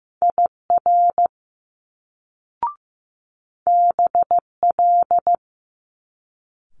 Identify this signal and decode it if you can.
Morse code, heard as a keyed tone, about 15 wpm, 700 Hz. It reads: IR BL
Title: Multi-letter Phonemes Quick Fire 15wpm